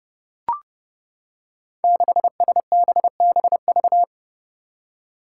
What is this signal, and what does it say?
Morse code, heard as a keyed tone, about 30 wpm, 700 Hz. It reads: 6SBBV
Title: Five Challenging Letter-Number Combinations 30wpm